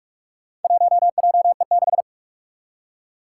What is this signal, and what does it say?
Morse code, heard as a keyed tone, about 46 wpm, 700 Hz. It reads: 1JE6
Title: Four Challenging Letter-Number Combinations 45wpm